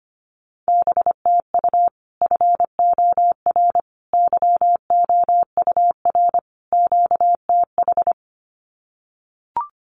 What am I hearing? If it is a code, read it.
Morse code, heard as a keyed tone, about 25 wpm, 700 Hz. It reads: BTU FOR YOUR QTH
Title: Common QSO Phrases 25wpm